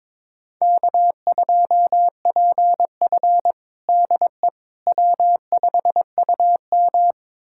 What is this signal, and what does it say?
Morse code, heard as a keyed tone, about 22 wpm, 700 Hz. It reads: K2PF DE W5UM
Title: USA Callsign DE Callsign - 4 Characters - 22wpm